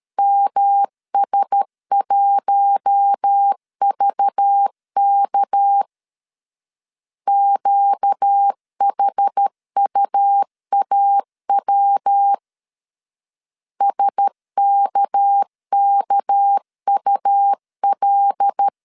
Morse code, heard as a keyed tone, about 13 wpm, 800 Hz. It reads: MS1VK QHUAW SKKUL
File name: LEZIONI / 6 PKX0 / 6 PK 12wpm 2006